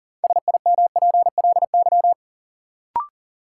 Morse code, heard as a keyed tone, about 38 wpm, 700 Hz. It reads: SIMPLY
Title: Sentences from Top 600 Words - New Word Drill - 40wpm